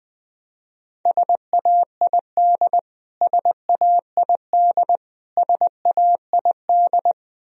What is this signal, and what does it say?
Morse code, heard as a keyed tone, about 20 wpm, 700 Hz. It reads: SAID SAID SAID
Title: Top 100 Words Repeated 20wpm